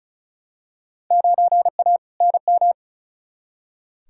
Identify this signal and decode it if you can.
Morse code, heard as a keyed tone, about 36 wpm, 700 Hz. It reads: 9A NM
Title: Field Day Exchanges Encore - 35wpm